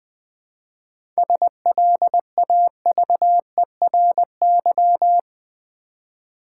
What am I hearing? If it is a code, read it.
Morse code, heard as a keyed tone, about 20 wpm, 700 Hz. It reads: SLAVERY
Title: 7 Letter Words - First 3 Letters Given - 20wpm